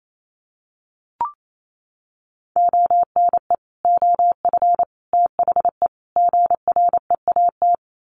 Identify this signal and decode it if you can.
Morse code, heard as a keyed tone, about 28 wpm, 700 Hz. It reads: ONE OF THE GREAT
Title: Sets of 4 Words 28wpm